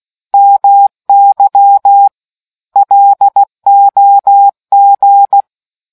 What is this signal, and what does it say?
Morse code, heard as a keyed tone, about 16 wpm, 800 Hz. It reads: MY LOG